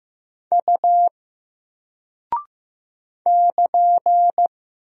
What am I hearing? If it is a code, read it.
Morse code, heard as a keyed tone, about 15 wpm, 700 Hz. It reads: U <KN>
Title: Single Character (Letter:Number:Punctuation:Prosign) Alt 15wpm